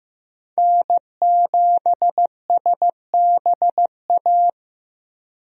Morse code, heard as a keyed tone, about 15 wpm, 700 Hz. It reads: N7SBA
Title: USA Call Signs - 5 Characters - Repeated 3x - 15wpm